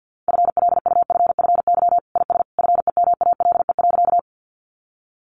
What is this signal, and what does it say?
Morse code, heard as a keyed tone, about 50 wpm, 700 Hz. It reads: FLUFFY IS FEMALE.
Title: Easy Phrases 50wpm